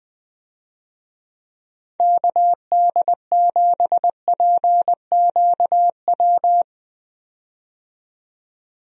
Morse code, heard as a keyed tone, about 20 wpm, 700 Hz. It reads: KD7PQW
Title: USA Call Signs - 6 Characters - 20wpm